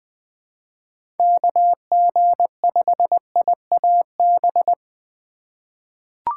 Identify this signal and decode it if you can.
Morse code, heard as a keyed tone, about 20 wpm, 700 Hz. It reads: KG5IAB
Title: USA Call Signs - 4x Character Spacing - 20wpm